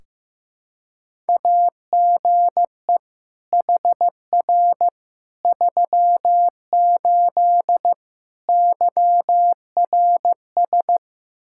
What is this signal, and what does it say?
Morse code, heard as a keyed tone, about 16 wpm, 700 Hz. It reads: AGE HR 38 YRS
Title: Practice QSO - Exchange 3 - More Info - 15wpm